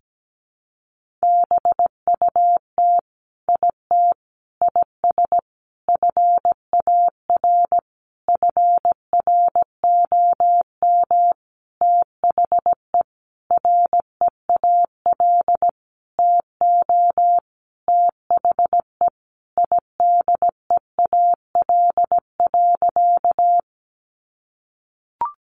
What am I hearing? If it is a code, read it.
Morse code, heard as a keyed tone, about 17 wpm, 700 Hz. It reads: BUT IT IS FAR FROM THE REAL TO THE IDEAL.
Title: Sentences from Top 1400 Words 17wpm